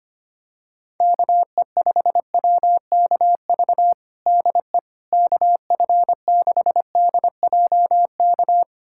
Morse code, heard as a keyed tone, about 25 wpm, 700 Hz. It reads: KE5WKV DE KF6DJK
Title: USA Callsign DE Callsign - 6 Characters - 25wpm